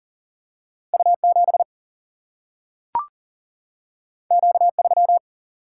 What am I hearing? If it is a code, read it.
Morse code, heard as a keyed tone, about 40 wpm, 700 Hz. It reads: U 7 Q 3
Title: Single-Letter Single-Number 40wpm